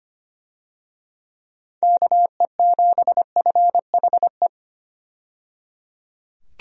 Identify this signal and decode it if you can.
Morse code, heard as a keyed tone, about 25 wpm, 700 Hz. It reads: KE7FHE
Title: USA Call Signs - 6 Characters - 25wpm